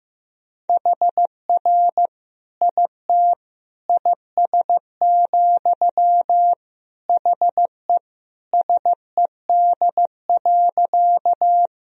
Morse code, heard as a keyed tone, about 15 wpm, 700 Hz. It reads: HR IT IS, HE SED.
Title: Sentences from Top 200 Words and Abbreviations 15wpm